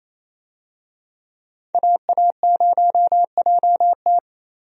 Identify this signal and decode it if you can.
Morse code, heard as a keyed tone, about 28 wpm, 700 Hz. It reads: AA0JT
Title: USA Call Signs - 5 Characters - Repeated 2x - 28wpm